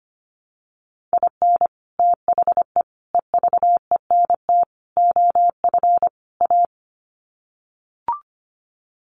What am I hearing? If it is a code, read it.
Morse code, heard as a keyed tone, about 25 wpm, 700 Hz. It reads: IN THE EVENT OF A
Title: Sets of 5 Words Encore 25wpm